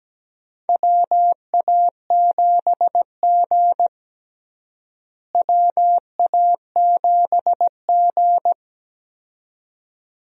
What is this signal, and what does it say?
Morse code, heard as a keyed tone, about 17 wpm, 700 Hz. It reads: WA7G WA7G
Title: USA Call Signs - 4 Characters - Repeated 2x - 17wpm